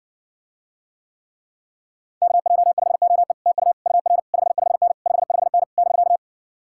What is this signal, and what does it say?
Morse code, heard as a keyed tone, about 59 wpm, 700 Hz. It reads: KJ5OE TU UR 55N 55N <BK>